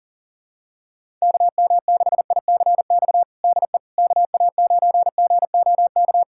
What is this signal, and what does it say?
Morse code, heard as a keyed tone, about 40 wpm, 700 Hz. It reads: KM6ICX DE KA9GOK
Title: USA Callsign DE Callsign - 6 Characters - 40wpm